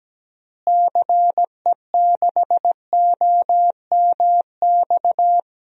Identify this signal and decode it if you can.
Morse code, heard as a keyed tone, about 17 wpm, 700 Hz. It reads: CE6OMX
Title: DXCC-100 Call Signs - 6 Characters - Repeated 2x - 17wpm